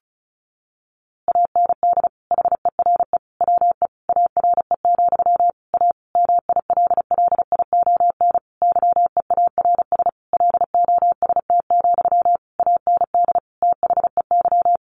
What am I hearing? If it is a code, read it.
Morse code, heard as a keyed tone, about 34 wpm, 700 Hz. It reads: AND HERE WE ARE, A MILLION YEARS LOST, AND THEY
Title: A Sound Of Thunder 35wpm